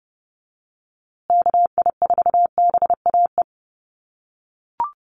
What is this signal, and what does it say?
Morse code, heard as a keyed tone, about 30 wpm, 700 Hz. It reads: KI4BAE
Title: USA Call Signs Repeated 2x 30wpm